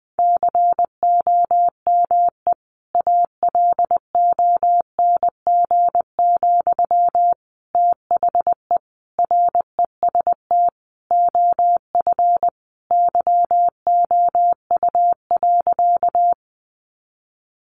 Morse code, heard as a keyed tone, about 20 wpm, 700 Hz. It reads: COME ALONG, THE REST OF YOU.
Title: Sentences from Top 700 Words 20wpm